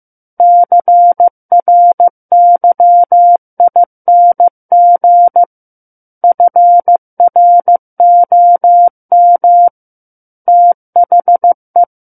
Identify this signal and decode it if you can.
Morse code, heard as a keyed tone, about 15 wpm, 700 Hz. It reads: CRYING FROM THE
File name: Book One / 06-Chapter 6 - 15 WPM